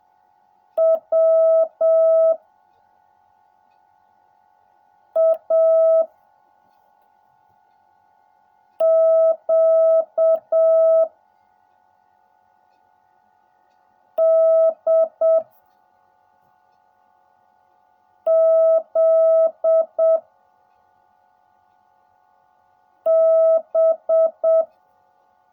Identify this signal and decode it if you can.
Morse code, heard as a keyed tone, about 7 wpm, 660 Hz. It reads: W A Q D Z B
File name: jh8cbh / 2021cw / cw16a